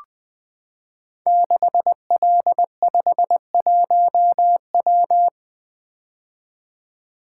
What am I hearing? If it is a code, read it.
Morse code, heard as a keyed tone, about 20 wpm, 700 Hz. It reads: 6L51W
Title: Code Groups - 5 Letter:Numbers - 20wpm